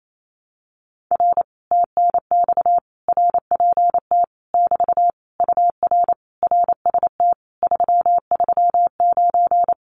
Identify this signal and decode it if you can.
Morse code, heard as a keyed tone, about 28 wpm, 700 Hz. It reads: R TNX RPT <BT> UR RST 339
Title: Common QSO Phrases Encore 28wpm